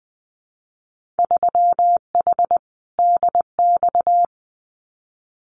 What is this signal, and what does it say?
Morse code, heard as a keyed tone, about 20 wpm, 700 Hz. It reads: 3H DX
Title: Winter Field Day Exchanges 20wpm